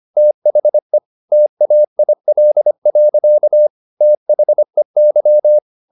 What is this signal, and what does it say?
Morse code, heard as a keyed tone, about 25 wpm, 600 Hz. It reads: THE TAIL. THEY
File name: the old man and the sea part 1 25wpm